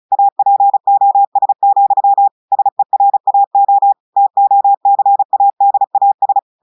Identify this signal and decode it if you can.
Morse code, heard as a keyed tone, about 35 wpm, 800 Hz. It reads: APOS, SERAO TOCADAS
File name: portugais texte en clair 2 vitesse 35 mots minutes